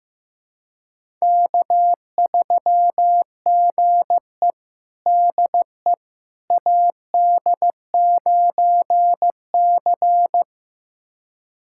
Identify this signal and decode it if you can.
Morse code, heard as a keyed tone, about 15 wpm, 700 Hz. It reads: K3GE DE AD9C
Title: USA Callsign DE Callsign - 4 Characters - 15wpm